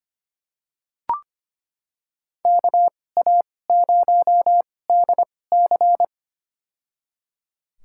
Morse code, heard as a keyed tone, about 25 wpm, spaced as for 18 wpm, 700 Hz. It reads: KA0DC
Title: USA Call Signs - 5 Characters - 2x Character Spacing - 25wpm